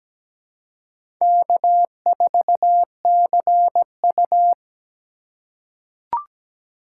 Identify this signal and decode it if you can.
Morse code, heard as a keyed tone, about 17 wpm, 700 Hz. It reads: K4CU
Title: USA Call Signs - 4x Character Spacing - 17wpm